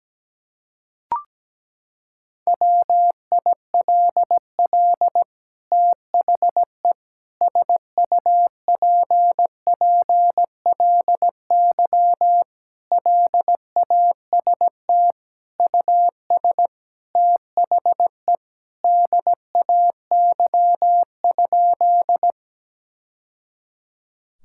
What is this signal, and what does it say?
Morse code, heard as a keyed tone, about 17 wpm, 700 Hz. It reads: WILL THE SUPPLY LAST US THE DAY?
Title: Sentences from Top 1000 Words - Review Part 5 - 17wpm